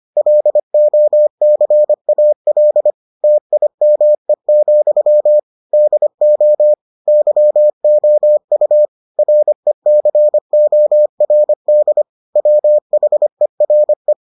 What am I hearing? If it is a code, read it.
Morse code, heard as a keyed tone, about 25 wpm, 600 Hz. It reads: LOCAL TIME, DO YOU RECORD WHERE